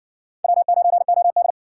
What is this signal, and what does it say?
Morse code, heard as a keyed tone, about 58 wpm, 700 Hz. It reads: W9OD